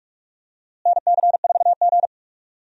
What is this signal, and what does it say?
Morse code, heard as a keyed tone, about 42 wpm, 700 Hz. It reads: NC4G
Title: USA Call Signs - 4 Characters - Repeated 2x - 45wpm